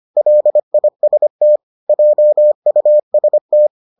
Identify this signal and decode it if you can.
Morse code, heard as a keyed tone, about 25 wpm, 600 Hz. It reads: LIST JUST